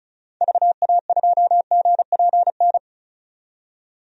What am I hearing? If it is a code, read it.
Morse code, heard as a keyed tone, about 34 wpm, 700 Hz. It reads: VA2GPN
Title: DXCC-100 Call Signs - 6 Characters - 35wpm